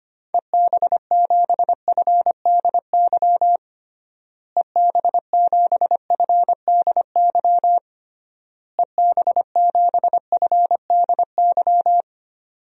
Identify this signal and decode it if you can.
Morse code, heard as a keyed tone, about 25 wpm, 700 Hz. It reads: EB7FDY EB7FDY EB7FDY
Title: DXCC-100 Call Signs - 6 Characters - Repeated 3x - 25wpm